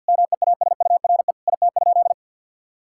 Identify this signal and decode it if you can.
Morse code, heard as a keyed tone, about 50 wpm, 700 Hz. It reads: MEASURE IT?
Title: Sentences from Top 1000 Words - Review Part 6 - 50wpm